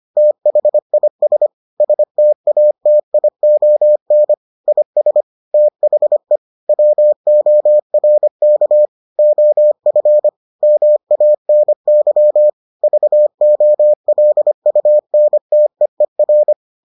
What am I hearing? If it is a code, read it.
Morse code, heard as a keyed tone, about 25 wpm, 600 Hz. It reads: THIS STATION IS THE WORK OF MANY VOLUNTEER